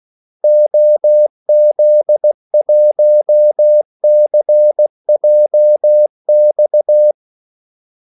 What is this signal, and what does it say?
Morse code, heard as a keyed tone, about 16 wpm, 600 Hz. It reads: OZ1CJX
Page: Kaldesignaler 16 wpm